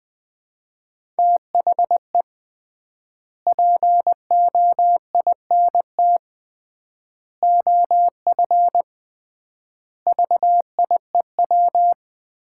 Morse code, heard as a keyed tone, about 20 wpm, 700 Hz. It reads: THE POINT OF VIEW
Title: 3x Word-Spacing - Sets of 5 Words - 20wpm